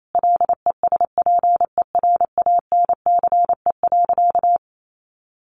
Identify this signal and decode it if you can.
Morse code, heard as a keyed tone, about 28 wpm, 700 Hz. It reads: LESPERANCE.
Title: A Sound Of Thunder 28wpm